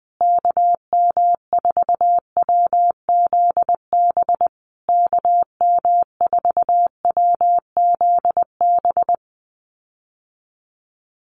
Morse code, heard as a keyed tone, about 20 wpm, 700 Hz. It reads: KM4WZB KM4WZB
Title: USA Call Signs Repeated 3x 20wpm